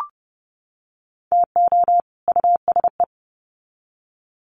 Morse code, heard as a keyed tone, about 30 wpm, 700 Hz. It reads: TO USE
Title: Sets of 2 Words 30wpm